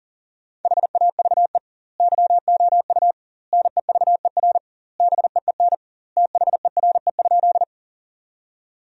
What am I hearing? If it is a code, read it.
Morse code, heard as a keyed tone, about 40 wpm, 700 Hz. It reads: HAVE YOU NEVER BEEN THERE?
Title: 2x Word-Spacing - Sentences from Top 300 Words - 40wpm